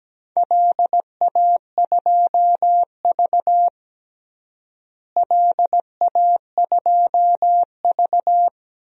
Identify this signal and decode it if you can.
Morse code, heard as a keyed tone, about 17 wpm, 700 Hz. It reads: LA2V LA2V
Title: DXCC-100 Call Signs - 4 Characters - Repeated 3x - 17wpm